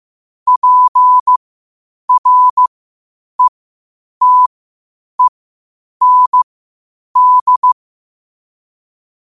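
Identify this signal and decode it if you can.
Morse code, heard as a keyed tone, about 15 wpm, spaced as for 8 wpm, 1000 Hz. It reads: PRETEND
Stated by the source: Morse code quotes at 8 WPM.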